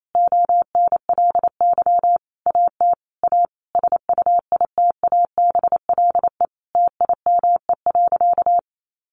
Morse code, heard as a keyed tone, about 28 wpm, 700 Hz. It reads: ONLY AT A SUITABLE TIME.
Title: Sentences from Top 1400 Words 28wpm